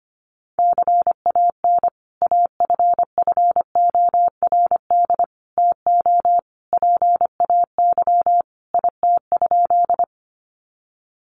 Morse code, heard as a keyed tone, about 25 wpm, 690 Hz. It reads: CAN AFFORD TO PAY IT?
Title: Sentences from Top 1100 Words 25wpm